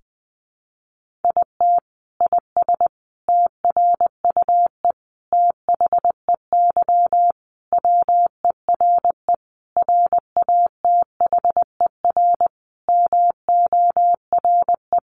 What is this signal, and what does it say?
Morse code, heard as a keyed tone, about 20 wpm, 700 Hz. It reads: IT IS TRUE THEY WERE RATHER MORE
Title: Sentences from Top 1400 Words 20wpm